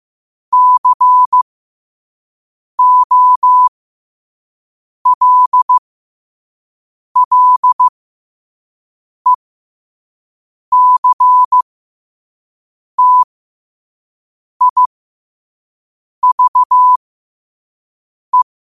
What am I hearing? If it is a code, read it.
Morse code, heard as a keyed tone, about 15 wpm, spaced as for 5 wpm, 1000 Hz. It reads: COLLECTIVE
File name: qotdfiles / 5wpm